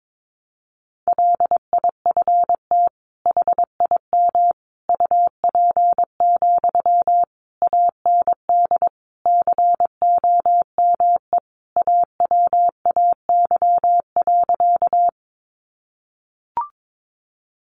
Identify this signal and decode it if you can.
Morse code, heard as a keyed tone, about 22 wpm, 700 Hz. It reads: LIFT HIM UP, AND COME AWAY.
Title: Sentences from Top 1000 Words 22wpm